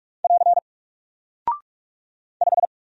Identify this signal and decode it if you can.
Morse code, heard as a keyed tone, about 44 wpm, 700 Hz. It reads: <AR> 5
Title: Single Character (Letter:Number:Punctuation:Prosign) 45wpm